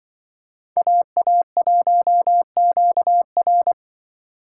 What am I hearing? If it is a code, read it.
Morse code, heard as a keyed tone, about 23 wpm, 700 Hz. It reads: AA1QR
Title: Speed Racing - USA 2x2 Call Signs 20wpm